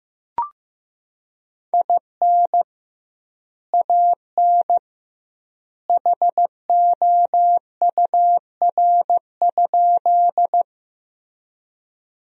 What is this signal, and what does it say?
Morse code, heard as a keyed tone, about 15 wpm, 700 Hz. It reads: IN AN HOUR?
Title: 2x Word-Spacing - Sentences from Top 400 Words - 15wpm